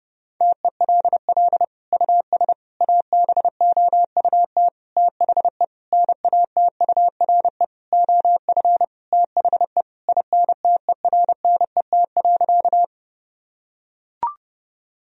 Morse code, heard as a keyed tone, about 30 wpm, 700 Hz. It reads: TELL US ABOUT THE NATURE OF THE INTERNET.
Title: Sentences from Top 1000 Words - Review Part 4 - 30wpm